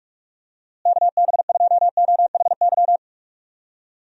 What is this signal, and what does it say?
Morse code, heard as a keyed tone, about 46 wpm, 700 Hz. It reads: KB2OHY
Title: USA Call Signs - 6 Characters - Repeated 3x - 45wpm